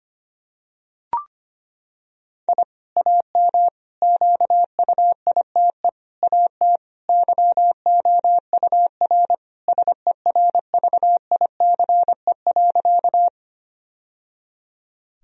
Morse code, heard as a keyed tone, about 25 wpm, 700 Hz. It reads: I AM QUITE AT YOUR SERVICE.
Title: Sentences from Top 1000 Words - Review Part 4 - 25wpm